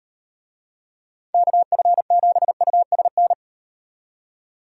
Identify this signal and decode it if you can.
Morse code, heard as a keyed tone, about 38 wpm, 700 Hz. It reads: KF7USN